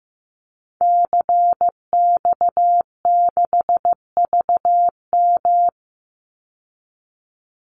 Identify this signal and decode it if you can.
Morse code, heard as a keyed tone, about 15 wpm, 700 Hz. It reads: CX6VM
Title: Call Signs Repeated 2x - 15wpm